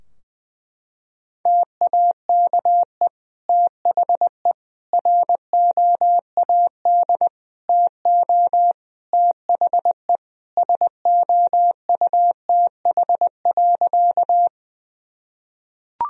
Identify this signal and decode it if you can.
Morse code, heard as a keyed tone, about 21 wpm, 700 Hz. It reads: TAKE THE ROAD TO THE SOUTH.
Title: Sentences from Top 1000 Words - Review Part 2 - 20wpm